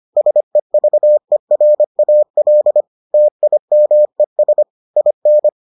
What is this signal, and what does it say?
Morse code, heard as a keyed tone, about 25 wpm, 600 Hz. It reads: SEVERAL TIMES IN